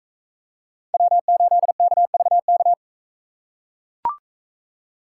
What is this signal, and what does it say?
Morse code, heard as a keyed tone, about 42 wpm, 700 Hz. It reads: W8KVK